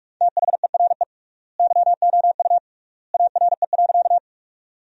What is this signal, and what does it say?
Morse code, heard as a keyed tone, about 41 wpm, 700 Hz. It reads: THERE YOU ARE.
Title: 3x Word-Spacing - Sentences from Top 100 Words - 45wpm